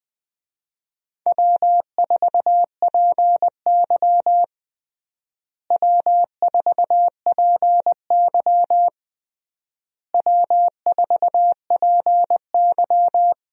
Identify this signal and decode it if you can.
Morse code, heard as a keyed tone, about 20 wpm, 700 Hz. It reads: W4PY W4PY W4PY
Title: USA Call Signs - 4 Characters - Repeated 3x - 20wpm